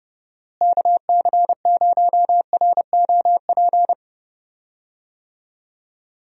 Morse code, heard as a keyed tone, about 31 wpm, 700 Hz. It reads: KC0ROP
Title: USA Call Signs - 6 Characters - Repeated 2x - 30wpm